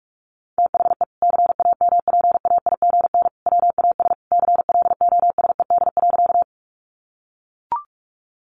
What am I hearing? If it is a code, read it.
Morse code, heard as a keyed tone, about 44 wpm, 700 Hz. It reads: THE CAMPAIGN WAS CLOSED.
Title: Sentences from Top 1400 Words 45wpm